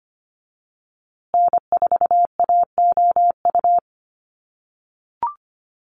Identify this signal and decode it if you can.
Morse code, heard as a keyed tone, about 25 wpm, 700 Hz. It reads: N4AOU
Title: USA Call Signs Repeated 2x 25wpm